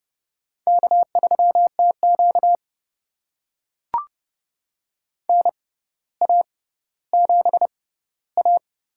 Morse code, heard as a keyed tone, about 30 wpm, spaced as for 15 wpm, 700 Hz. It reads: K3TQ NA7A
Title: USA Call Signs - 4 Characters - 6x Character Spacing - 30wpm